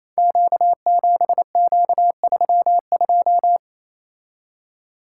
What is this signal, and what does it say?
Morse code, heard as a keyed tone, about 28 wpm, 700 Hz. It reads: Q7Q32
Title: Code Groups - 5 Letter:Numbers - 28wpm